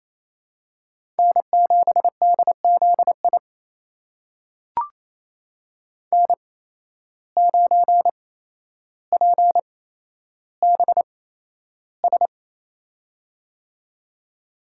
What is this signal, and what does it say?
Morse code, heard as a keyed tone, about 28 wpm, spaced as for 12 wpm, 700 Hz. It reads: N7DZI N9PBS
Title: USA Call Signs - 5 Characters - 8x Character Spacing - 28wpm